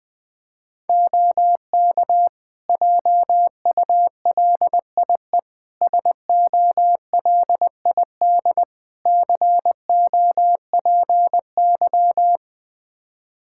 Morse code, heard as a keyed tone, about 20 wpm, 700 Hz. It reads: OK JULIE SOLID COPY
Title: Practice QSO - Exchange 3 - More Info - 20wpm